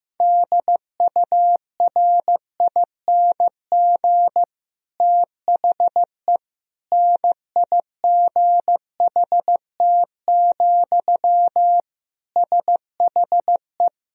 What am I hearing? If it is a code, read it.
Morse code, heard as a keyed tone, about 15 wpm, 700 Hz. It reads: DURING THE NIGHT, SHE
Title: Sentences from Top 1000 Words - Review Part 2 - 15wpm